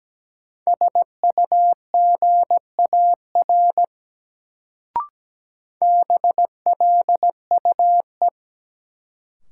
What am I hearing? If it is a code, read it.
Morse code, heard as a keyed tone, about 17 wpm, 700 Hz. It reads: SUGAR BLUE
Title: Sentences from Top 1200 Words - New Word Drill Alt - 17wpm